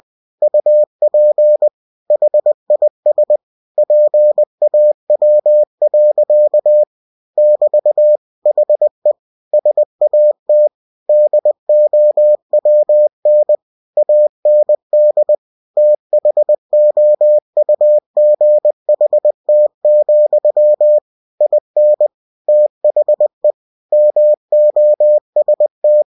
Morse code, heard as a keyed tone, about 20 wpm, 600 Hz. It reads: UP HIS PAW. <BT> HE SAT DOWN AND THOUGHT, IN THE MOST
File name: cw-books / winnie-the-pooh / winnie-20wpm-page-0025